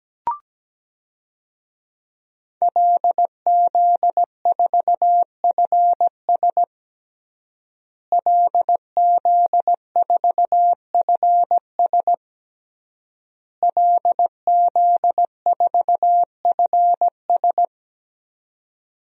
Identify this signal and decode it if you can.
Morse code, heard as a keyed tone, about 17 wpm, 700 Hz. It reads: LZ4FS LZ4FS LZ4FS
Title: DXCC-100 Call Signs - 5 Characters - Repeated 3x - 17wpm